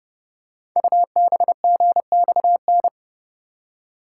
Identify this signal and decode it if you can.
Morse code, heard as a keyed tone, about 30 wpm, 700 Hz. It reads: UBGXN
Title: Code Groups - 5 Letters - 30wpm